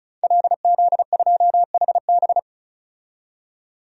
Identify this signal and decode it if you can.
Morse code, heard as a keyed tone, about 34 wpm, 700 Hz. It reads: LZ2HB
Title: DXCC-100 Call Signs - 5 Characters - 35wpm